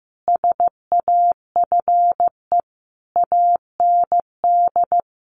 Morse code, heard as a keyed tone, about 15 wpm, 700 Hz. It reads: SAFE AND
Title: Sentences from Top 900 Words 15wpm